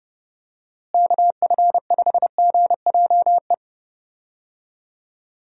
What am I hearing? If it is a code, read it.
Morse code, heard as a keyed tone, about 30 wpm, 700 Hz. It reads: KF5GJE
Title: USA Call Signs - 6 Characters - Repeated 3x - 30wpm